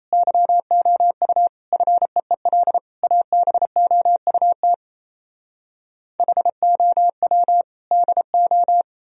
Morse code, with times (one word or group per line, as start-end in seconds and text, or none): 0.1-1.5 YOU
1.7-2.8 FEEL
3.0-4.7 ABOUT
6.2-7.6 HOW
7.9-8.8 DO